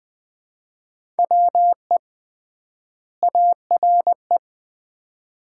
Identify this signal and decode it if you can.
Morse code, heard as a keyed tone, about 21 wpm, 700 Hz. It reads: WE ARE
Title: 3x Word-Spacing - Sets of 2 Words - 20wpm